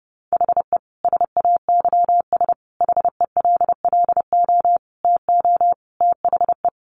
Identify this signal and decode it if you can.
Morse code, heard as a keyed tone, about 30 wpm, 700 Hz. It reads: HE SAYS HELLO TO THE
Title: Additional Easy Phrases 30wpm